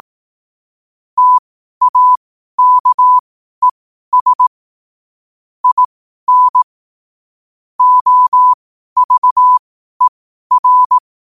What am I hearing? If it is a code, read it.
Morse code, heard as a keyed tone, about 18 wpm, spaced as for 12 wpm, 1000 Hz. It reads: TAKES IN OVER
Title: 12 WPM morse code quotes for Fri, 01 Mar 2024